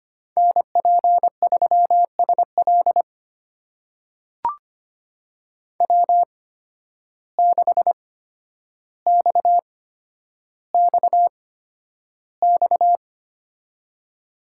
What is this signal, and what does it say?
Morse code, heard as a keyed tone, about 25 wpm, spaced as for 11 wpm, 700 Hz. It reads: NP3SL W6XXX
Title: USA Call Signs - 5 Characters - 8x Character Spacing - 25wpm